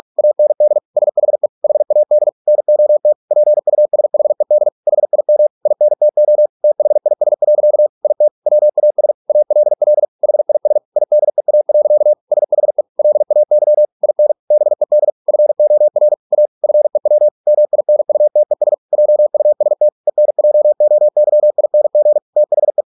Morse code, heard as a keyed tone, about 46 wpm, 600 Hz. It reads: AND SHE HAD NOT PUSHED HIM INTO THIS. IT WAS ALL HIS IDEA. SHE LAY IN BED FOR A FEW MINUTES JUST ENJOYING THE